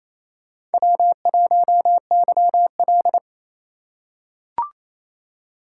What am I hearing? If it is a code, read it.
Morse code, heard as a keyed tone, about 27 wpm, 700 Hz. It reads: W1YL
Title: USA Call Signs - 4 Characters - 3x Character Spacing - 28wpm